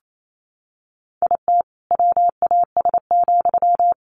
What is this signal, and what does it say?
Morse code, heard as a keyed tone, about 28 wpm, 700 Hz. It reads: IT WAS,
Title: Sentences from Top 1400 Words 28wpm